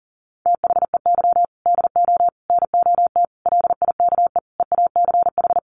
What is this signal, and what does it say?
Morse code, heard as a keyed tone, about 40 wpm, 700 Hz. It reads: THEY DO NOT LIKE EACH
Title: Easy Phrases 40wpm